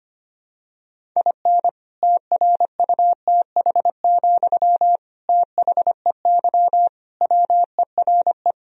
Morse code, heard as a keyed tone, about 25 wpm, 700 Hz. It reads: IN TRUTH, THEY WERE
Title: Sentences from Top 1000 Words - Review Part 1 - 25wpm